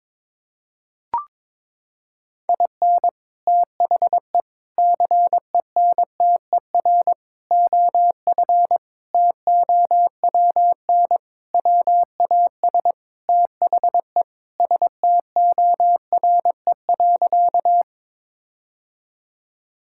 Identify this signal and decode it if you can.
Morse code, heard as a keyed tone, about 22 wpm, 700 Hz. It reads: IN THE CENTER OF TOWN WAS THE STORE.
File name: practice / 22wpm / Sentences with 4-Word Phrases - 22wpm